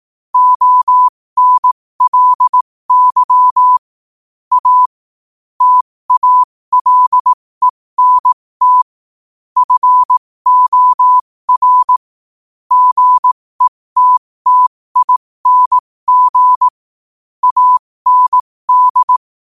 Morse code, heard as a keyed tone, about 18 wpm, spaced as for 15 wpm, 1000 Hz. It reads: ONLY A TALENT FOR GETTING AND
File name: qotdfiles / 15wpm